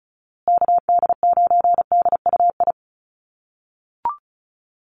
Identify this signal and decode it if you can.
Morse code, heard as a keyed tone, about 34 wpm, 700 Hz. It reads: KD9DUI
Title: USA Call Signs Repeated 3x 35wpm